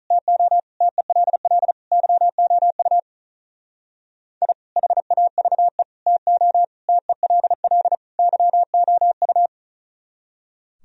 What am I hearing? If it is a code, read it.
Morse code, heard as a keyed tone, about 44 wpm, 700 Hz. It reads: TO TELL YOU I HAVE TO TELL YOU